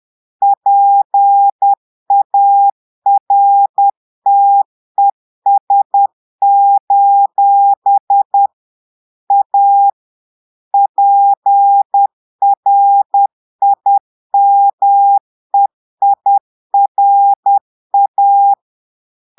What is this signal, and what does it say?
Morse code, heard as a keyed tone, about 10 wpm, 800 Hz. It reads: PARTES: A PRIMEIRA
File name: Portugais - texte en clair 1 - vitesse - 10 mots minute